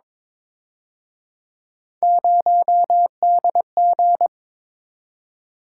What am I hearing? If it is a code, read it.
Morse code, heard as a keyed tone, about 26 wpm, 700 Hz. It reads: 0DG
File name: Speed-Racing - Three Letter-Number Combinations - 22wpm